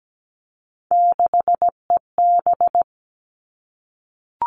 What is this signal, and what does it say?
Morse code, heard as a keyed tone, about 17 wpm, 700 Hz. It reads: 6EB
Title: Three Challenging Letter-Number Combinations 17wpm